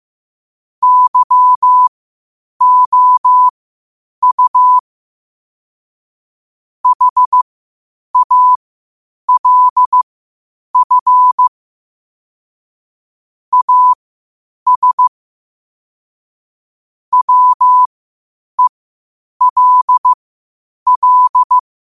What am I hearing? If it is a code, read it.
Morse code, heard as a keyed tone, about 15 wpm, spaced as for 8 wpm, 1000 Hz. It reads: YOU HALF AS WELL
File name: qotdfiles / 8wpm